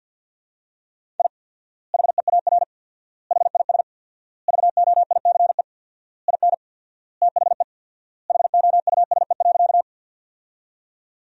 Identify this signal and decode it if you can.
Morse code, heard as a keyed tone, about 49 wpm, 700 Hz. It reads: I HEAR HIS VOICE IN THE HOUSE.
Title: 4x Word-Spacing - Sentences from Top 500 Words - 50wpm